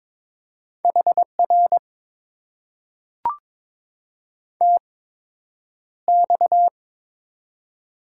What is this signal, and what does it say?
Morse code, heard as a keyed tone, about 22 wpm, spaced as for 9 wpm, 700 Hz. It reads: HR TX
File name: Top 90 QSO Elements - 8x Character Spacing - 22wpm